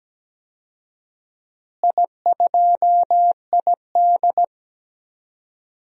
Morse code, heard as a keyed tone, about 17 wpm, 700 Hz. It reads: I2ID
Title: DXCC-100 Call Signs - 4 Characters - Repeated 3x - 17wpm